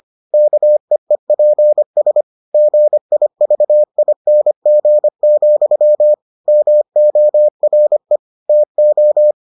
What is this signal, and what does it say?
Morse code, heard as a keyed tone, about 25 wpm, 600 Hz. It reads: KEEPS GIVING, MORE TO